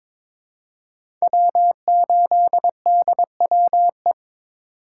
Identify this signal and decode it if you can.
Morse code, heard as a keyed tone, about 22 wpm, 700 Hz. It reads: W8DWE
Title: USA Call Signs - 5 Characters - 2x Character Spacing - 22wpm